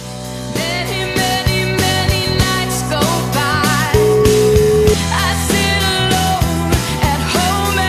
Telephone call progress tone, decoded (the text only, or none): ringback tone